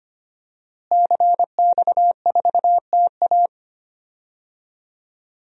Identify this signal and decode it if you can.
Morse code, heard as a keyed tone, about 25 wpm, 700 Hz. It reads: CX4TA
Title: DXCC-100 Call Signs - 5 Characters - Repeated 3x - 25wpm